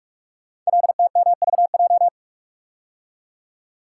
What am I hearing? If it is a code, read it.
Morse code, heard as a keyed tone, about 44 wpm, 690 Hz. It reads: LTMVJ
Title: Code Groups - 5 Letters - 45wpm